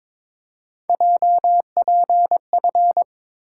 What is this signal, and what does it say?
Morse code, heard as a keyed tone, about 26 wpm, 700 Hz. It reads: JPF
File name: Speed-Racing - Three Letter-Number Combinations - 22wpm